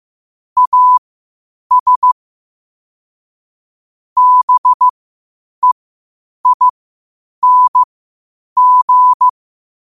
Morse code, heard as a keyed tone, about 15 wpm, spaced as for 8 wpm, 1000 Hz. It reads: AS BEING
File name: qotdfiles / 8wpm